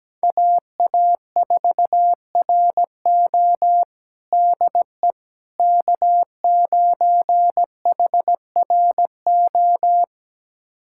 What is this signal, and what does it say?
Morse code, heard as a keyed tone, about 17 wpm, 700 Hz. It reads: AA4RO DE K9HRO
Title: USA Callsign DE Callsign - 5 Characters - 17wpm